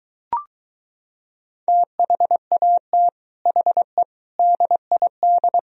Morse code, heard as a keyed tone, about 23 wpm, 700 Hz. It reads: THAT HE DID